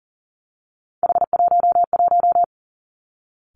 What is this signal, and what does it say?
Morse code, heard as a keyed tone, about 40 wpm, 700 Hz. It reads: H11
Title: Three Challenging Letter-Number Combinations 40wpm